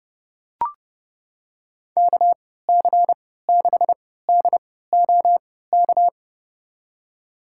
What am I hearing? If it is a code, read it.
Morse code, heard as a keyed tone, about 30 wpm, spaced as for 17 wpm, 700 Hz. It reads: KC6DOK
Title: USA Call Signs - 6 Characters - 3x Character Spacing - 30wpm